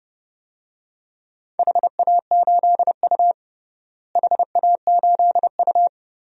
Morse code, heard as a keyed tone, about 30 wpm, 700 Hz. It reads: HA8U HA8U
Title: DXCC-100 Call Signs - 4 Characters - Repeated 2x - 30wpm